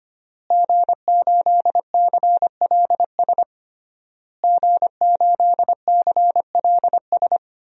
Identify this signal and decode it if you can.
Morse code, heard as a keyed tone, about 25 wpm, 700 Hz. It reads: G8CLS G8CLS
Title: DXCC-100 Call Signs - 5 Characters - Repeated 3x - 25wpm